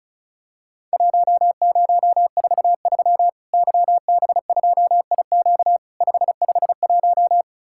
Morse code, heard as a keyed tone, about 34 wpm, 700 Hz. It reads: 1043 YB2IQ 551
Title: Worked All Europe DX Contest - QTC Exchange - 35wpm